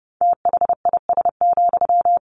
Morse code, heard as a keyed tone, about 30 wpm, 700 Hz. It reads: THIS,
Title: Sentences from Top 1200 Words 30wpm